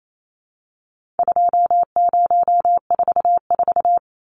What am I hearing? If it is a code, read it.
Morse code, heard as a keyed tone, about 28 wpm, 700 Hz. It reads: 2044
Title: Numbers - 4-Digit 28wpm